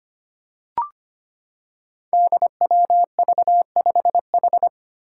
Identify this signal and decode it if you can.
Morse code, heard as a keyed tone, about 25 wpm, 700 Hz. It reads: DWV5H
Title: Five Challenging Letter-Number Combinations 25wpm